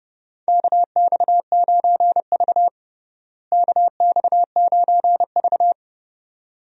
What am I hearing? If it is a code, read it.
Morse code, heard as a keyed tone, about 30 wpm, 700 Hz. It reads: KX9V KX9V
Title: USA Call Signs - 4 Characters - Repeated 2x - 30wpm